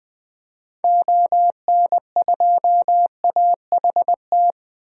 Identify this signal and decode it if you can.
Morse code, heard as a keyed tone, about 20 wpm, 700 Hz. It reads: ON2AHT
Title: DXCC-100 Call Signs - 6 Characters - 20wpm